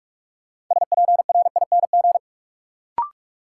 Morse code, heard as a keyed tone, about 46 wpm, 700 Hz. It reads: SPRING
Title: Sentences from Top 1200 Words - New Word Drill Alt - 45wpm